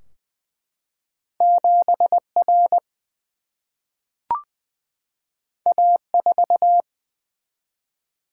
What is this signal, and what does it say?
Morse code, heard as a keyed tone, about 20 wpm, 700 Hz. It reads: 7 R A 4
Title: Single-Letter Single-Number 20wpm